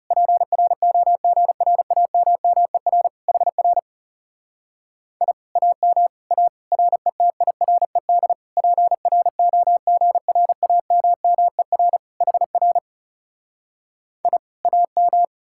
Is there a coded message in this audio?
Yes: Morse code from 0.1 to 15.2 s, heard as a keyed tone, about 40 wpm, 700 Hz. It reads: PROGRAMMER HR I AM A RETIRED PROGRAMMER HR I AM